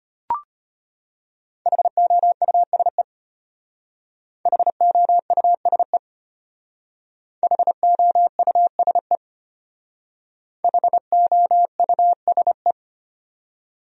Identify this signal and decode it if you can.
Morse code, heard as a keyed tone, about 32 wpm, 700 Hz. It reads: HOUSE HOUSE HOUSE HOUSE